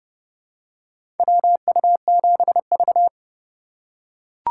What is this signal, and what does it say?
Morse code, heard as a keyed tone, about 30 wpm, 700 Hz. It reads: WU7V
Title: USA Call Signs - 4 Characters - 2x Character Spacing - 30wpm